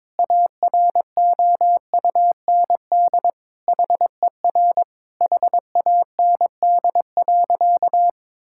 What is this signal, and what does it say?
Morse code, heard as a keyed tone, about 22 wpm, 700 Hz. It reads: AROUND HER HAND.
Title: Sentences from Top 1600 Words 22wpm